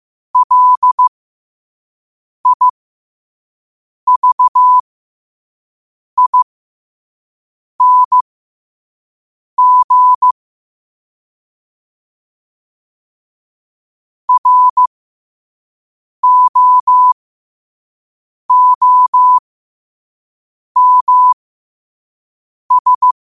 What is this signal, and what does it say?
Morse code, heard as a keyed tone, about 15 wpm, spaced as for 5 wpm, 1000 Hz. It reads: LIVING ROOMS
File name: qotdfiles / 5wpm